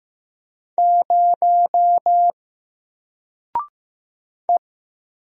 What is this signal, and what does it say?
Morse code, heard as a keyed tone, about 15 wpm, 700 Hz. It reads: 0 E
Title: Single Character (Letter:Number:Punctuation:Prosign) Alt 15wpm